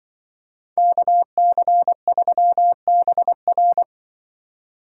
Morse code, heard as a keyed tone, about 23 wpm, 700 Hz. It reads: KC3BR
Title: Speed Racing - USA Call Signs 20wpm